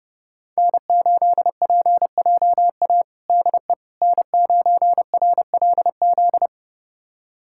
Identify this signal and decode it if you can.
Morse code, heard as a keyed tone, about 30 wpm, 700 Hz. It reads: N8PJA DE N9RLZ
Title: USA Callsign DE Callsign - 5 Characters - 30wpm